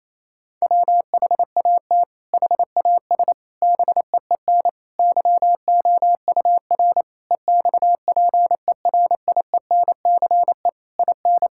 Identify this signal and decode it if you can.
Morse code, heard as a keyed tone, about 28 wpm, 700 Hz. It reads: WHAT HAS BEEN YOUR EXPERIENCE IN
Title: Sentences from Top 1000 Words - Review Part 6 - 28wpm